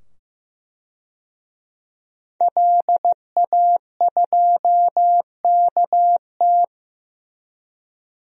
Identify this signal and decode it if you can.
Morse code, heard as a keyed tone, about 15 wpm, 700 Hz. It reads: LA2KT
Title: DXCC-100 Call Signs - 5 Characters - 15wpm